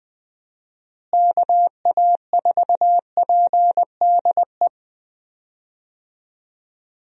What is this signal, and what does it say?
Morse code, heard as a keyed tone, about 20 wpm, 700 Hz. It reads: KA4PDE
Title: USA Call Signs - 6 Characters - Repeated 2x - 20wpm